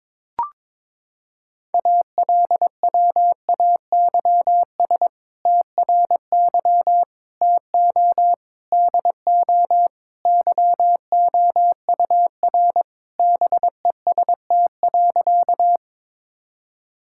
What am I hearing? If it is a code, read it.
Morse code, heard as a keyed tone, about 22 wpm, 700 Hz. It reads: ALWAYS TRY TO DO YOUR BEST.
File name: practice / 22wpm / Sentences with 2-Word Phrases - 22wpm